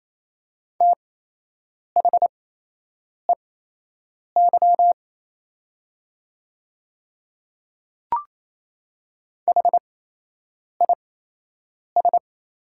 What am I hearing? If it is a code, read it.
Morse code, heard as a keyed tone, about 27 wpm, spaced as for 8 wpm, 690 Hz. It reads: THEY HIS
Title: 8x Character Spacing - Top 100 Words - 28wpm